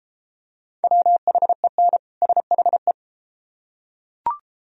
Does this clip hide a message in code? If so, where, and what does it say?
Yes: Morse code from 0.8 to 2.9 s, heard as a keyed tone, about 31 wpm, 700 Hz. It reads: WHEN SHE